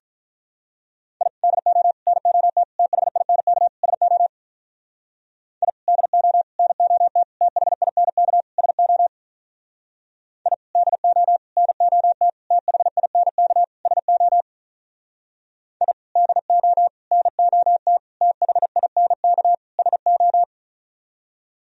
Morse code, heard as a keyed tone, about 45 wpm, 700 Hz. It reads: I DO NOT THINK SO I DO NOT THINK SO I DO NOT THINK SO I DO NOT THINK SO